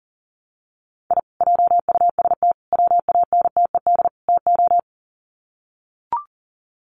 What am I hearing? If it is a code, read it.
Morse code, heard as a keyed tone, about 40 wpm, 700 Hz. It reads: I JUST WANTED TO
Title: Sets of 4 Words 40wpm